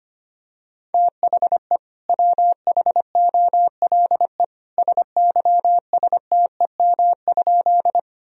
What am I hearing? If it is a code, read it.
Morse code, heard as a keyed tone, about 25 wpm, 700 Hz. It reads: THE WHOLE SYSTEM?
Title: Sentences from Top 1000 Words - Review Part 3 - 25wpm